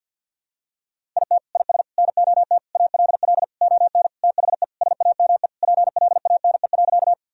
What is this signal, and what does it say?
Morse code, heard as a keyed tone, about 49 wpm, 700 Hz. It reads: IT IS NOT ALL ON THE SAME PLANE.
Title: Sentences from Top 1000 Words - Review Part 3 - 50wpm